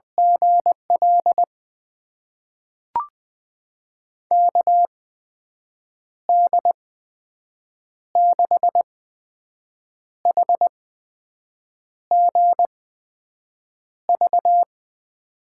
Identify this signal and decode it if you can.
Morse code, heard as a keyed tone, about 20 wpm, spaced as for 6 wpm, 700 Hz. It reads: GL KD6HGV
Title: USA Call Signs - 6 Characters - 8x Character Spacing - 20wpm